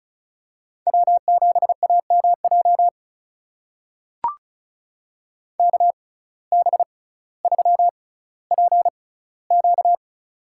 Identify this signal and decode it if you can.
Morse code, heard as a keyed tone, about 36 wpm, spaced as for 18 wpm, 700 Hz. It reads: W7AMJ KB3PQ
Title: USA Call Signs - 5 Characters - 6x Character Spacing - 35wpm